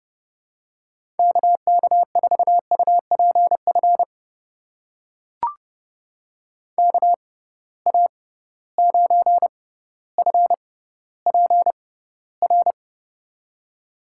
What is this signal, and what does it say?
Morse code, heard as a keyed tone, about 30 wpm, spaced as for 15 wpm, 700 Hz. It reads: KK4UPF KA9FPR
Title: USA Call Signs - 6 Characters - 6x Character Spacing - 30wpm